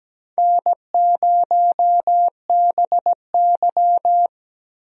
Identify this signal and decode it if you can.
Morse code, heard as a keyed tone, about 17 wpm, 700 Hz. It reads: N0BY
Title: USA Call Signs - 4 Characters - 17wpm